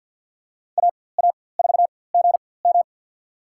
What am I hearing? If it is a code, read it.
Morse code, heard as a keyed tone, about 48 wpm, spaced as for 23 wpm, 700 Hz. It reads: AA4GM
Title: USA Call Signs - 5 Characters - 4x Character Spacing - 50wpm